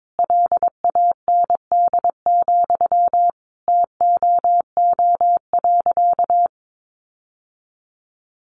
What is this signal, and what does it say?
Morse code, heard as a keyed tone, about 22 wpm, 700 Hz. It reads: LAND, TOO.
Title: Sentences from Top 1500 Words 22wpm